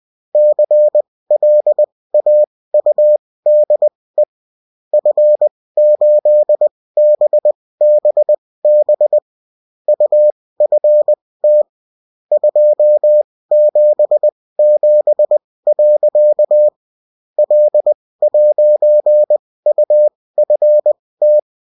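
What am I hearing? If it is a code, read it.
Morse code, heard as a keyed tone, about 20 wpm, spaced as for 16 wpm, 600 Hz. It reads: CLAUDE F8BBB UFT 277. L'UFT
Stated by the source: Bulletin in text and CW form.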